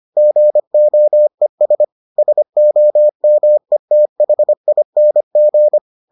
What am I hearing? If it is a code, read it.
Morse code, heard as a keyed tone, about 25 wpm, 600 Hz. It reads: GOES SOMETHING